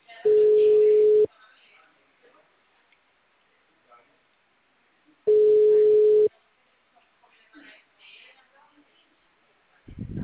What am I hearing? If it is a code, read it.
Ringback tone (Europe, 425 Hz), with speech.